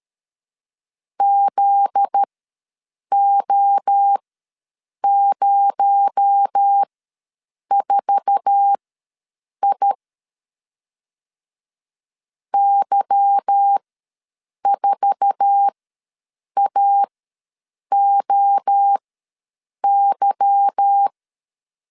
Morse code, heard as a keyed tone, about 13 wpm, spaced as for 7 wpm, 800 Hz. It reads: ZO04I Y4AOY
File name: LEZIONI / 8 YJ45 / 8 YJ4 ws 12wpm 3249